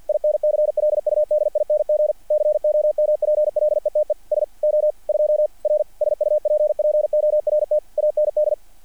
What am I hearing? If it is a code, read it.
Morse code, heard as a keyed tone, about 48 wpm, 600 Hz. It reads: INCLUDING COMPLETE SO2R SUPPORT AND
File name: archived / CRSnet